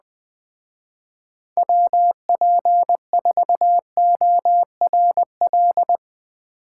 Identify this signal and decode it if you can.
Morse code, heard as a keyed tone, about 21 wpm, 700 Hz. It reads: WP4ORL